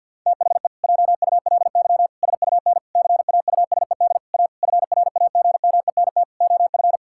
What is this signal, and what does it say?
Morse code, heard as a keyed tone, about 52 wpm, 700 Hz. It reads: THE JULY SUN CAUSED A FRAGMENT OF
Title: Pangram Challenge v2 - 50wpm